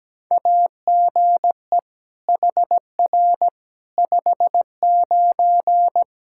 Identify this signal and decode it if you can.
Morse code, heard as a keyed tone, about 17 wpm, 700 Hz. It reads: AGE HR 59
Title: K1USN SST QSO Practice 17wpm